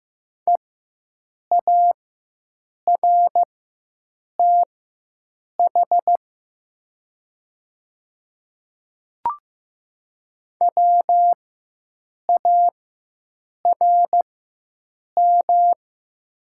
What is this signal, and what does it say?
Morse code, heard as a keyed tone, about 15 wpm, spaced as for 7 wpm, 700 Hz. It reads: EARTH WARM
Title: 4x Character Spacing - Top 500 Words - 15wpm